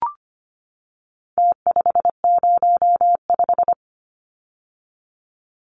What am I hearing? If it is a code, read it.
Morse code, heard as a keyed tone, about 25 wpm, 700 Hz. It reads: T505
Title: Numbers - Contest Style - 4-Digit 25wpm